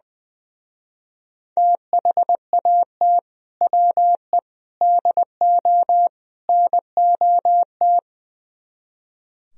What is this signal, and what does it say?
Morse code, heard as a keyed tone, about 22 wpm, 700 Hz. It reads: THAT WE DO NOT
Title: Speed-Racing - Sets of 4 Words 20wpm